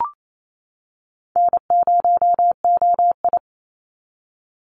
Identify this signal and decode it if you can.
Morse code, heard as a keyed tone, about 28 wpm, 700 Hz. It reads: N0OI
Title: NA SOTA Activator Callsigns 28wpm